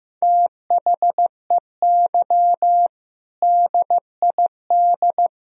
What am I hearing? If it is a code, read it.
Morse code, heard as a keyed tone, about 15 wpm, 700 Hz. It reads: THEY DID
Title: Speed-Racing - Sets of 5 Words 15wpm